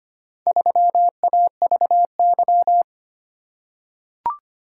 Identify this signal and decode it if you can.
Morse code, heard as a keyed tone, about 25 wpm, 700 Hz. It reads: 3AVY
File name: practice / 25wpm / Four Letter-Number Combinations - 25wpm